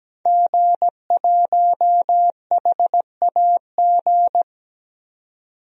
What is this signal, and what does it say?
Morse code, heard as a keyed tone, about 17 wpm, 700 Hz. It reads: G1HAG
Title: DXCC-100 Call Signs - 5 Characters - 17wpm